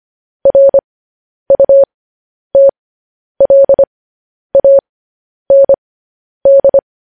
Morse code, heard as a keyed tone, about 25 wpm, spaced as for 10 wpm, 550 Hz. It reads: RUTLAND
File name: podcast / News headlines-1x-25x10